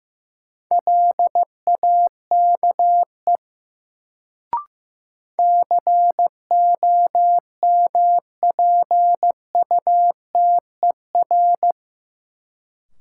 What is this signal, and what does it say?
Morse code, heard as a keyed tone, about 15 wpm, 700 Hz. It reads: LAKE COMPUTER
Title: Sentences from Top 1000 Words - Review Part 3 - New Word Drill - 15wpm